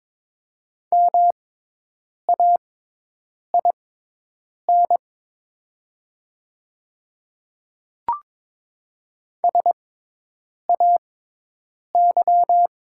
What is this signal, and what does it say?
Morse code, heard as a keyed tone, about 22 wpm, spaced as for 8 wpm, 700 Hz. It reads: MAIN SAY
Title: 6x Character Spacing - Top 500 Words - 22wpm